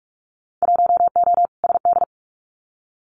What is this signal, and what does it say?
Morse code, heard as a keyed tone, about 46 wpm, 700 Hz. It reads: 1O SD
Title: Winter Field Day Exchanges Encore 45wpm